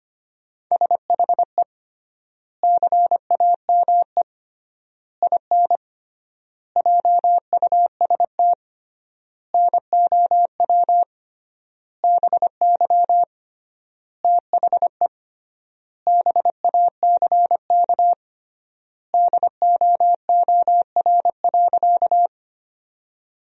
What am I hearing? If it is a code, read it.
Morse code, heard as a keyed tone, about 25 wpm, 700 Hz. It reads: SHE CAME IN JUST NOW BY THE BACK DOOR.
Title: 3x Word-Spacing - Sentences from Top 300 Words - 25wpm